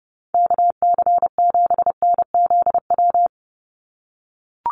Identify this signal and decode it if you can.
Morse code, heard as a keyed tone, about 30 wpm, 700 Hz. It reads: KC7NZW
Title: USA Call Signs 30wpm